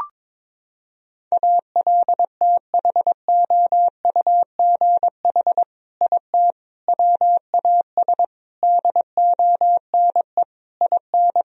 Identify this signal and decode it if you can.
Morse code, heard as a keyed tone, about 22 wpm, 700 Hz. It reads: ALTHOUGH IT WAS DONE IN
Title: Sentences from Top 1000 Words - Review Part 3 - 22wpm